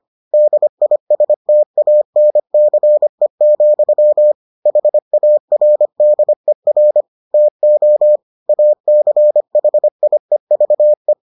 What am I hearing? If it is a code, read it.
Morse code, heard as a keyed tone, about 25 wpm, 600 Hz. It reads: DISTANCE, HARDER TO ACHIEVE